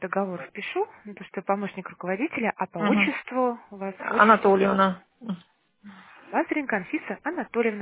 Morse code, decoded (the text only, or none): none